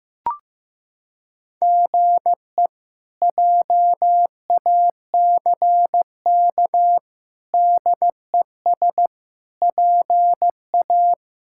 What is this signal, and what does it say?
Morse code, heard as a keyed tone, about 15 wpm, 700 Hz. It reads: GE JACK DES PA
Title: K1USN SST QSO Practice - 15wpm